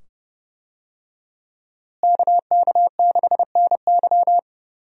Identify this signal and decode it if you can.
Morse code, heard as a keyed tone, about 30 wpm, 700 Hz. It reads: KK6NY
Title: USA Call Signs - 5 Characters - Repeated 2x - 30wpm